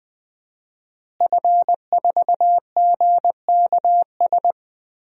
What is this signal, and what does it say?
Morse code, heard as a keyed tone, about 20 wpm, 700 Hz. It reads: F4GKS
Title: DXCC-100 Call Signs - 5 Characters - Repeated 2x - 20wpm